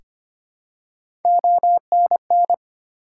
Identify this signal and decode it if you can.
Morse code, heard as a keyed tone, about 25 wpm, 700 Hz. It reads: ONN
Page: morse-code-ninja-practice-25wpm